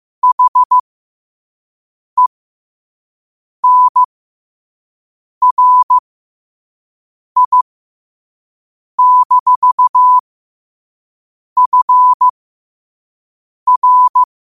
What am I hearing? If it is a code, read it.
Morse code, heard as a keyed tone, about 15 wpm, spaced as for 5 wpm, 1000 Hz. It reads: HENRI-FR